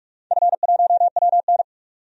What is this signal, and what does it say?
Morse code, heard as a keyed tone, about 46 wpm, 700 Hz. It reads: F1WN
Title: DXCC-100 Call Signs - 4 Characters - 45wpm